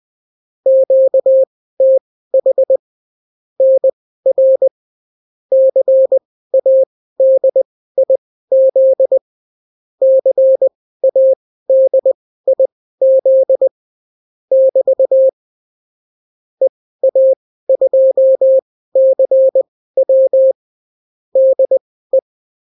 Morse code, heard as a keyed tone, about 20 wpm, spaced as for 14 wpm, 530 Hz. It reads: QTH NR CADIZ CADIZ <BT> EA2CW DE
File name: CW QSO libroCW4 20-10ppm